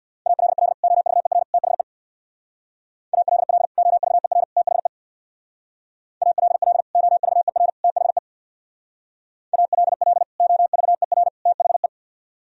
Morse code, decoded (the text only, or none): ALL OVER THE ALL OVER THE ALL OVER THE ALL OVER THE